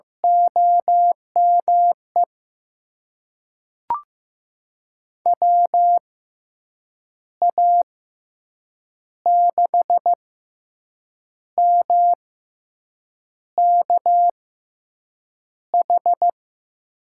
Morse code, heard as a keyed tone, about 15 wpm, spaced as for 6 wpm, 700 Hz. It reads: OME WA6MKH
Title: USA Call Signs - 6 Characters - 6x Character Spacing - 15wpm